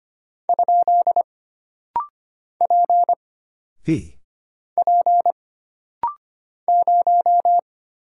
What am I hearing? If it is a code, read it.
Morse code, heard as a keyed tone, about 25 wpm, 700 Hz. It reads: ? P P 0
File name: Single Character (Letter:Number:Punctuation:Prosign) - Rapid-Fire - 25wpm